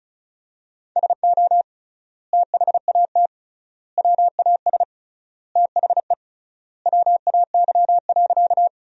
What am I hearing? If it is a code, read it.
Morse code, heard as a keyed tone, about 34 wpm, 700 Hz. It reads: SO THAT WAS THE WAY.
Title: 3x Word-Spacing - Sentences from Top 100 Words - 35wpm